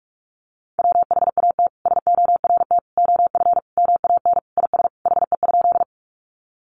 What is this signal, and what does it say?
Morse code, heard as a keyed tone, about 44 wpm, 700 Hz. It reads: WHAT SORT OF MAN IS HE?
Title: Sentences from Top 1300 Words 45wpm